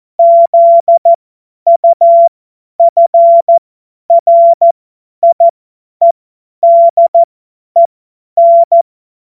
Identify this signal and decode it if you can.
Morse code, heard as a keyed tone, about 14 wpm, spaced as for 10 wpm, 680 Hz. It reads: ZUFRIEDEN
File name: Morsekurs / Witze / Witz 13. Dezember 2022 50 F70